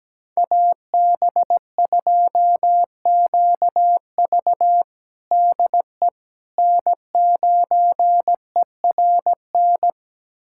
Morse code, heard as a keyed tone, about 17 wpm, 700 Hz. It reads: AB2QV DE N9ERN
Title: USA Callsign DE Callsign - 5 Characters - 17wpm